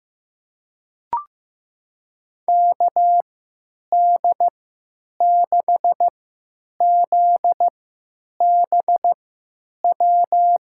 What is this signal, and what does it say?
Morse code, heard as a keyed tone, about 15 wpm, spaced as for 9 wpm, 700 Hz. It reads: KD6ZBW
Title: USA Call Signs - 6 Characters - 3x Character Spacing - 15wpm